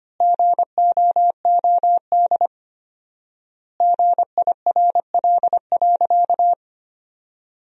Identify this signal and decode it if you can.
Morse code, heard as a keyed tone, about 25 wpm, 700 Hz. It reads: GOOD GIRL.
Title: 4x Word-Spacing - Sentences from Top 400 Words - 25wpm